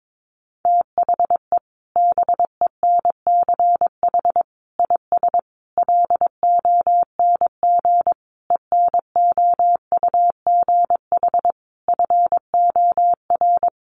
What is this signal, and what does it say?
Morse code, heard as a keyed tone, about 22 wpm, 700 Hz. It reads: THE BENCH IS LONG ENOUGH FOR
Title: Sentences from Top 1400 Words 22wpm